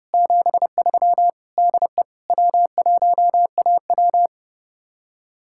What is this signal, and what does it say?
Morse code, heard as a keyed tone, about 36 wpm, 700 Hz. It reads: 73 DE W1AW
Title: Speed-Racing Common QSO Phrases - 30wpm